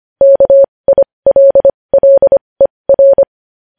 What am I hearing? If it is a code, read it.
Morse code, heard as a keyed tone, about 25 wpm, spaced as for 20 wpm, 550 Hz. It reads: KILLER
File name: podcast / News headlines-1x-25x20